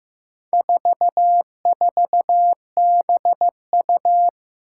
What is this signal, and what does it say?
Morse code, heard as a keyed tone, about 15 wpm, 700 Hz. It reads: 44BU
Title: Four Challenging Letter-Number Combinations 15wpm